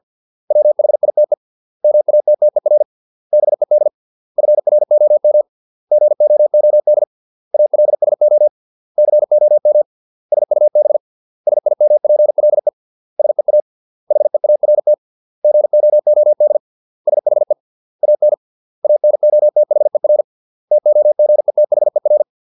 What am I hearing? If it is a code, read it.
Morse code, heard as a keyed tone, about 50 wpm, 600 Hz. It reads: WHITE MATTER BED FROM GOOD ALSO COM SUB SIMPLE SEA HEART GOOD SHE AN ANOTHER TOGETHER